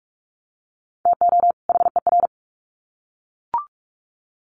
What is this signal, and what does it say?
Morse code, heard as a keyed tone, about 44 wpm, 700 Hz. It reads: TO HER
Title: Sets of 2 Words 45wpm